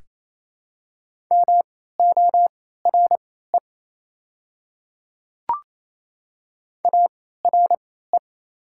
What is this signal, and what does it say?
Morse code, heard as a keyed tone, about 28 wpm, spaced as for 16 wpm, 700 Hz. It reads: MORE ARE
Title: 3x Character Spacing - Top 100 Words - 28wpm